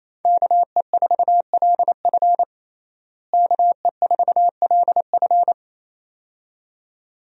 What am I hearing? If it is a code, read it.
Morse code, heard as a keyed tone, about 27 wpm, 700 Hz. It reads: KE4LF KE4LF
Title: USA Call Signs - 5 Characters - Repeated 2x - 28wpm